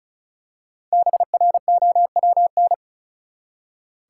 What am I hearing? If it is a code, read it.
Morse code, heard as a keyed tone, about 36 wpm, 700 Hz. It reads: BROWN
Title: Five Letter Words - 35wpm